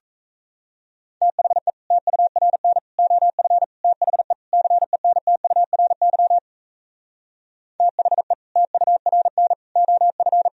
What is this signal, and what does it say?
Morse code, heard as a keyed tone, about 37 wpm, 700 Hz. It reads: THE TURN OF THE CENTURY THE TURN OF